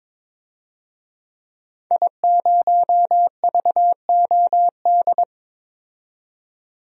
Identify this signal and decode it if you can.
Morse code, heard as a keyed tone, about 22 wpm, 700 Hz. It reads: I0VOD
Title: DXCC-100 Call Signs - 5 Characters - 22wpm